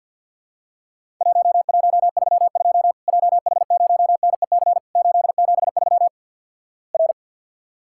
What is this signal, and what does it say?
Morse code, heard as a keyed tone, about 50 wpm, 700 Hz. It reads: 1132 JH0NEC 873 R
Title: Worked All Europe DX Contest - QTC Exchange with Repeats - 50wpm